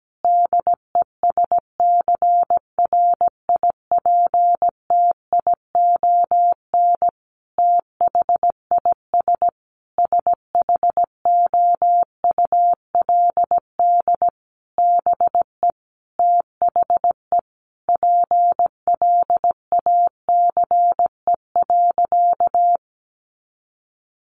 Morse code, heard as a keyed tone, about 17 wpm, 700 Hz. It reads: DESCRIPTION THIS SHOULD BE THE PLACE.
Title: Sentences from Top 1300 Words 17wpm